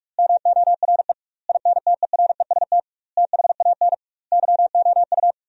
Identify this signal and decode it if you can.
Morse code, heard as a keyed tone, about 46 wpm, 700 Hz. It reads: MORE INTEREST THAN YOU
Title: 2x Word-Spacing - Sentences from Top 400 Words - 45wpm